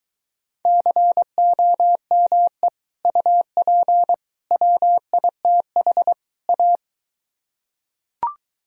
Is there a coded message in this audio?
Yes: Morse code from 0.7 to 6.8 s, heard as a keyed tone, about 22 wpm, 700 Hz. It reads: COME UP WITH A